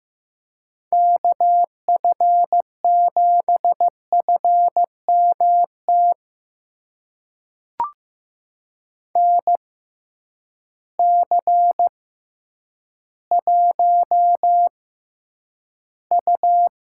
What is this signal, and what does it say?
Morse code, heard as a keyed tone, about 15 wpm, spaced as for 9 wpm, 700 Hz. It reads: KF7FMT NC1U
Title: USA Call Signs - 6x Character Spacing - 15wpm